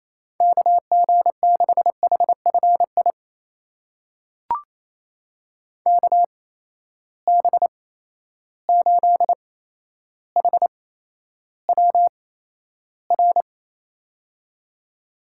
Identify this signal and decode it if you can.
Morse code, heard as a keyed tone, about 28 wpm, spaced as for 12 wpm, 700 Hz. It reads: KG6HFI KB8HWR
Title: USA Call Signs - 6 Characters - 8x Character Spacing - 28wpm